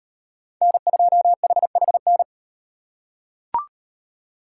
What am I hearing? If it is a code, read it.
Morse code, heard as a keyed tone, about 38 wpm, 700 Hz. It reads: N2HHN